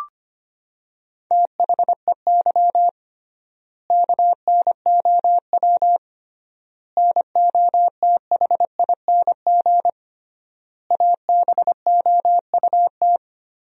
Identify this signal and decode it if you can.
Morse code, heard as a keyed tone, about 25 wpm, 700 Hz. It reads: THEY KNOW NOTHING ABOUT
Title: 3x Word-Spacing - Sentences from Top 500 Words - 25wpm